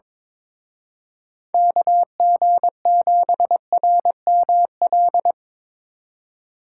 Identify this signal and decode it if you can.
Morse code, heard as a keyed tone, about 25 wpm, 700 Hz. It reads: KG7RML
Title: Speed Racing - USA 2x3 Call Signs 22wpm